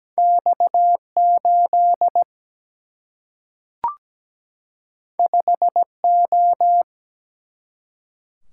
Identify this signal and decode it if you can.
Morse code, heard as a keyed tone, about 17 wpm, 700 Hz. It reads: X8 5O
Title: Single-Letter Single-Number 17wpm